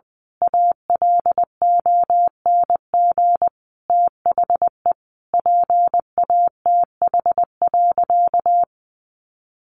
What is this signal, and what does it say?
Morse code, heard as a keyed tone, about 19 wpm, 700 Hz. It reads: ALONG THE PATH.
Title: A Sound Of Thunder 20wpm